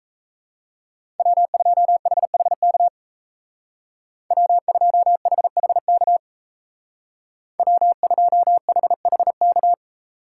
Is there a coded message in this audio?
Yes: Morse code from 1.2 to 9.7 s, heard as a keyed tone, about 36 wpm, 700 Hz. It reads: W2HHK W2HHK W2HHK